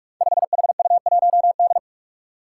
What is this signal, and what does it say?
Morse code, heard as a keyed tone, about 46 wpm, 700 Hz. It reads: 5HU1D
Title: Five Challenging Letter-Number Combinations 45wpm